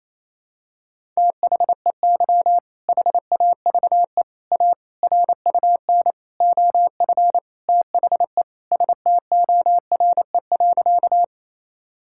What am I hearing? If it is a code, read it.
Morse code, heard as a keyed tone, about 28 wpm, 700 Hz. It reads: THEY HAVE A RUN OF THE STORE.
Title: Sentences from Top 1000 Words - Review Part 5 - 28wpm